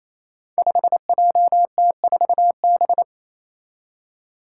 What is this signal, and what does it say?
Morse code, heard as a keyed tone, about 28 wpm, 700 Hz. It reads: 5JT4B
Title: Five Challenging Letter-Number Combinations 28wpm